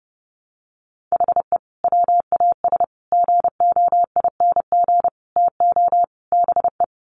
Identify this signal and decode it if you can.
Morse code, heard as a keyed tone, about 30 wpm, 700 Hz. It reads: HE WAS GOING TO BE
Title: Sets of 5 Words 30wpm